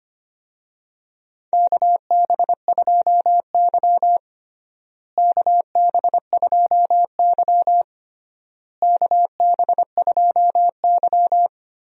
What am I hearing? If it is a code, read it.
Morse code, heard as a keyed tone, about 25 wpm, 700 Hz. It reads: KB2Y KB2Y KB2Y
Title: USA Call Signs - 4 Characters - Repeated 3x - 25wpm